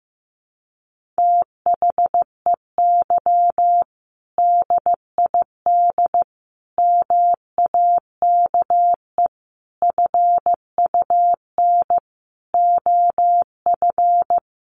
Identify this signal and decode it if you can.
Morse code, heard as a keyed tone, about 15 wpm, 700 Hz. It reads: THEY DID MAKE FUN OF
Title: Sentences from Top 1300 Words 15wpm